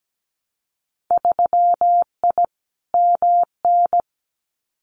Morse code, heard as a keyed tone, about 17 wpm, 700 Hz. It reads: 3I MN
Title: Winter Field Day Exchanges 17wpm